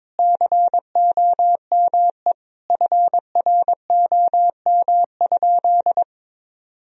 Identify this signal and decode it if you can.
Morse code, heard as a keyed tone, about 21 wpm, 700 Hz. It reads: COME FROM?
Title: Sentences from Top 1000 Words - Review Part 1 - 22wpm